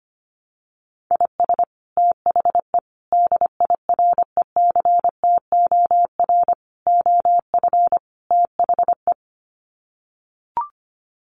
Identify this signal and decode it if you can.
Morse code, heard as a keyed tone, about 25 wpm, 700 Hz. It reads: IS THE DIRECTOR OF THE
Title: Sets of 5 Words Encore2 25wpm